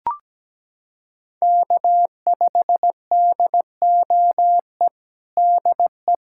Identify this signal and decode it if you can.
Morse code, heard as a keyed tone, about 17 wpm, 700 Hz. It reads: K5DOE DE
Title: USA Callsign DE Callsign - 5 Characters - 17wpm